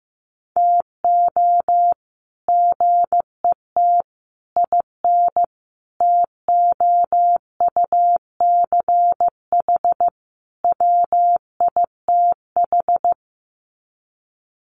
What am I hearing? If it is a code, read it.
Morse code, heard as a keyed tone, about 15 wpm, 700 Hz. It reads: TO GET IN TOUCH WITH
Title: Sets of 5 Words Encore2 15wpm